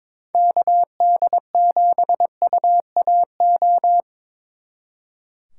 Morse code, heard as a keyed tone, about 26 wpm, 700 Hz. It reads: KD7UAO
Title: Speed Racing - USA 2x3 Call Signs 22wpm